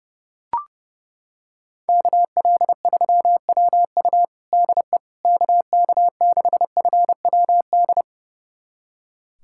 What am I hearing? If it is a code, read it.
Morse code, heard as a keyed tone, about 30 wpm, 690 Hz. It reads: KL3WU DE KK6FWD
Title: USA Callsign DE Callsign 30wpm